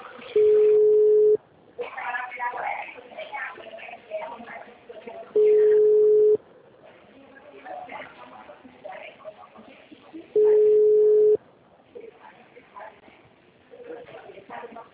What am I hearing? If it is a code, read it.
Ringback tone (Europe, 425 Hz), with speech.